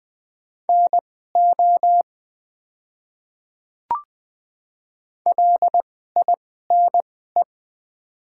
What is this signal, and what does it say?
Morse code, heard as a keyed tone, about 20 wpm, spaced as for 14 wpm, 690 Hz. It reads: NO LINE
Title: 2x Character Spacing - Top 500 Words - 20wpm